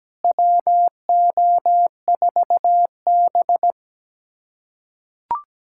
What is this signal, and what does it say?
Morse code, heard as a keyed tone, about 17 wpm, 700 Hz. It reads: WO4B
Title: USA Callsign DE Callsign - 4 Characters - 17wpm